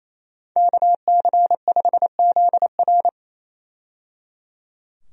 Morse code, heard as a keyed tone, about 28 wpm, 700 Hz. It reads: KC5ZR
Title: USA Call Signs - 5 Characters - 28wpm